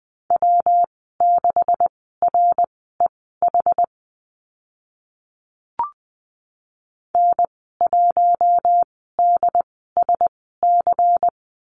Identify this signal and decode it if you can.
Morse code, heard as a keyed tone, about 20 wpm, spaced as for 15 wpm, 700 Hz. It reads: W6REH N1DSC
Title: 140 Lesson 23g - Letter f - Call Signs 20wpm